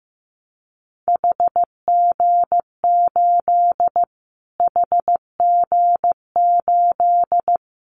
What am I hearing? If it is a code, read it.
Morse code, heard as a keyed tone, about 15 wpm, 700 Hz. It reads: HG8 HG8
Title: Callsign Prefixes for Top 25 DXCC - Repeated 3x - 15wpm